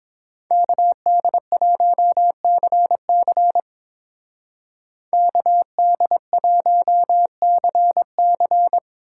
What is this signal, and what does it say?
Morse code, heard as a keyed tone, about 26 wpm, 700 Hz. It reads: KD1CC KD1CC
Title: Speed Racing - USA Call Signs 22wpm